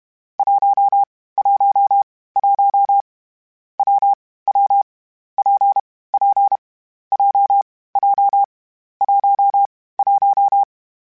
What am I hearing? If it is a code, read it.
Morse code, heard as a keyed tone, about 32 wpm, 800 Hz. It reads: 1 1 1 W W P P J J 1 1
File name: Difficult Combos / Difficult Combos-mp3s / WPJ WPJ1 @30